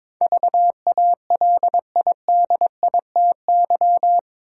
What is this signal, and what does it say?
Morse code, heard as a keyed tone, about 22 wpm, 700 Hz. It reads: VALIDITY
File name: practice / 22wpm / Eight Letter Words - 22wpm